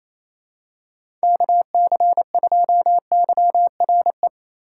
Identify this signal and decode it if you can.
Morse code, heard as a keyed tone, about 28 wpm, 700 Hz. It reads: KC2YRE
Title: USA Call Signs - 6 Characters - 3x Character Spacing - 28wpm